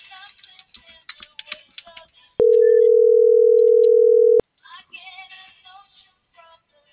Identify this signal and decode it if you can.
Ringback tone (North America), 440 + 480 Hz.
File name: CENTRAL SAGOT / out edgar.sagot - .2358